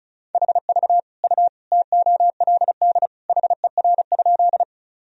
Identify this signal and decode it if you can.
Morse code, heard as a keyed tone, about 34 wpm, 700 Hz. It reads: HV U TOLD HER?
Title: Sentences from Top 400 Words and Abbreviations 35wpm